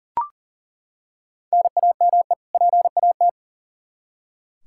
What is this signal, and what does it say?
Morse code, heard as a keyed tone, about 40 wpm, 700 Hz. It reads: NAME PAT
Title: Practice QSO - Exchange 1 - Beginning - 40wpm